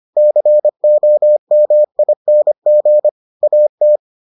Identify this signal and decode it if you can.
Morse code, heard as a keyed tone, about 25 wpm, 600 Hz. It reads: COMING AT